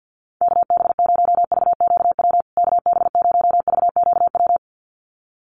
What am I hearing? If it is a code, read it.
Morse code, heard as a keyed tone, about 50 wpm, 700 Hz. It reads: KB0VQW KB0VQW
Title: USA Call Signs Repeated 2x 50wpm